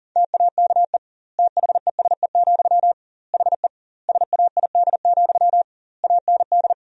Morse code, heard as a keyed tone, about 40 wpm, 700 Hz. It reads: TAKE THESE, HE SAID, AND
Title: 2x Word-Spacing - Sentences from Top 300 Words - 40wpm